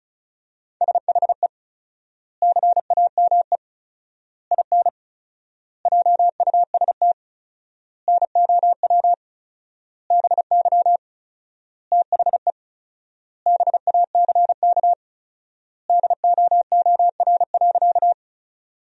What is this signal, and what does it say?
Morse code, heard as a keyed tone, about 34 wpm, 700 Hz. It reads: SHE CAME IN JUST NOW BY THE BACK DOOR.
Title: 4x Word-Spacing - Sentences from Top 300 Words - 35wpm